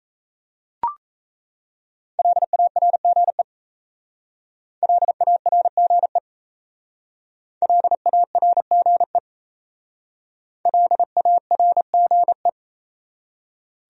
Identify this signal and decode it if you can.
Morse code, heard as a keyed tone, about 34 wpm, 700 Hz. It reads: LARGE LARGE LARGE LARGE
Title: Speed-Racing Top 500 Words 28wpm